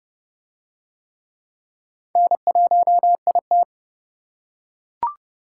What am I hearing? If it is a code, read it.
Morse code, heard as a keyed tone, about 29 wpm, 700 Hz. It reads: N1IT